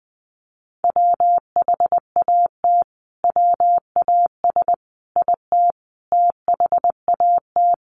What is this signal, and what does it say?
Morse code, heard as a keyed tone, about 20 wpm, 700 Hz. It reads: WHAT WAS IT THAT
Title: Sentences from Top 300 Words 20wpm